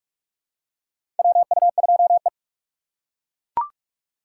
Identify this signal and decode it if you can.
Morse code, heard as a keyed tone, about 42 wpm, 700 Hz. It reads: WU2E
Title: USA Call Signs - 4 Characters - 2x Character Spacing - 45wpm